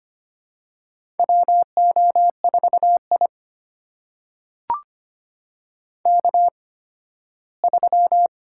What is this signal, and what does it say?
Morse code, heard as a keyed tone, about 25 wpm, spaced as for 15 wpm, 700 Hz. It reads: WO4I K3
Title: USA Call Signs - 4 Characters - 8x Character Spacing - 25wpm